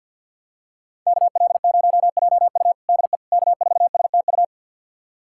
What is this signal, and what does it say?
Morse code, heard as a keyed tone, about 52 wpm, 700 Hz. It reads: KL0JU DE K4STU
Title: USA Callsign DE Callsign - 5 Characters - 50wpm